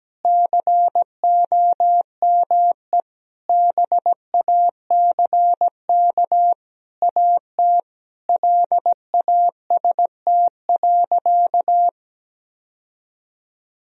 Morse code, heard as a keyed tone, about 17 wpm, 700 Hz. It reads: COME BACK AT LAST.
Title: Sentences from Top 1000 Words - Review Part 2 - 17wpm